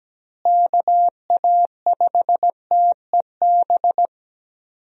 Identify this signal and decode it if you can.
Morse code, heard as a keyed tone, about 17 wpm, 700 Hz. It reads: KA5TEB
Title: USA Call Signs - 6 Characters - Repeated 2x - 17wpm